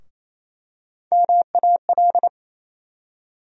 Morse code, heard as a keyed tone, about 28 wpm, 700 Hz. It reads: MAL
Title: USA Call Sign Postfixes - 3 Letters 28wpm